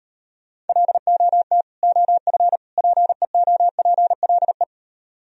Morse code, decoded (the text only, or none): LOT OF PEOPLE